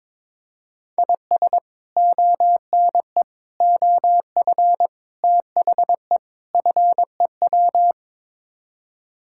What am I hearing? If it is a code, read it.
Morse code, heard as a keyed tone, about 22 wpm, 700 Hz. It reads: IS ONE OF THE FEW
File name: practice / 22wpm / Build A Phrase - 5-Word Phrases - 22wpm